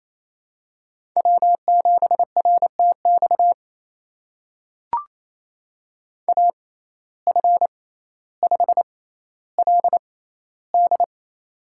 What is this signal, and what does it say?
Morse code, heard as a keyed tone, about 28 wpm, spaced as for 14 wpm, 700 Hz. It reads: W7RTX AF5LD
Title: USA Call Signs - 5 Characters - 6x Character Spacing - 28wpm